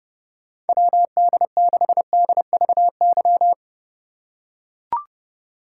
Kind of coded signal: Morse code

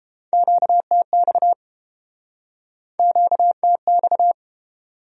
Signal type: Morse code

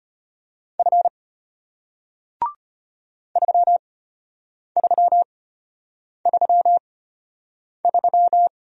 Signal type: Morse code

